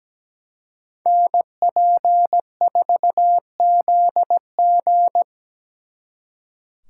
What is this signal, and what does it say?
Morse code, heard as a keyed tone, about 20 wpm, 700 Hz. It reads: NP4ZG
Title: Speed Racing - USA 2x2 Call Signs 17wpm